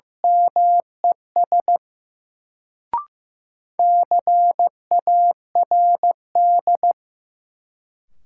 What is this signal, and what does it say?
Morse code, heard as a keyed tone, about 16 wpm, 700 Hz. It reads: MES CARD
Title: Sentences from Top 600 Words - New Word Drill - 15wpm